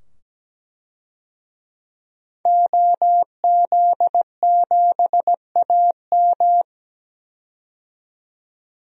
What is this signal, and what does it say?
Morse code, heard as a keyed tone, about 17 wpm, 700 Hz. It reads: OZ7AM
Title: DXCC-100 Call Signs - 5 Characters - 17wpm